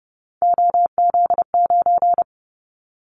Morse code, heard as a keyed tone, about 30 wpm, 700 Hz. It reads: OZ9
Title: Callsign Prefixes for Top 100 DXCC - 30wpm